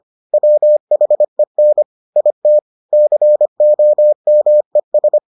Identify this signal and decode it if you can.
Morse code, heard as a keyed tone, about 25 wpm, 600 Hz. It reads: WHEN IT COMES